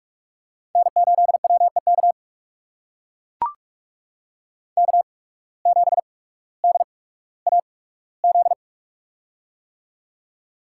Morse code, heard as a keyed tone, about 44 wpm, spaced as for 19 wpm, 690 Hz. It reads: N8WEK K7DAZ
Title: USA Call Signs - 5 Characters - 8x Character Spacing - 45wpm